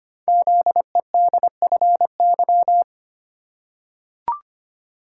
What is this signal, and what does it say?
Morse code, heard as a keyed tone, about 25 wpm, 700 Hz. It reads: ZEDFY
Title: Code Groups - 5 Letters - 25wpm